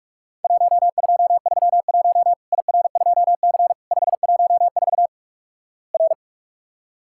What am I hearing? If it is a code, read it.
Morse code, heard as a keyed tone, about 48 wpm, 700 Hz. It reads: 1231 IR2C 514 R
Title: Worked All Europe DX Contest - QTC Exchange - 45wpm